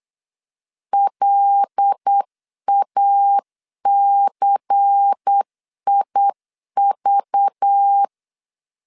Morse code, heard as a keyed tone, about 8 wpm, 800 Hz. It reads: LACIV
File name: CW / DET / 1 6 1500C S C 8wpm 5219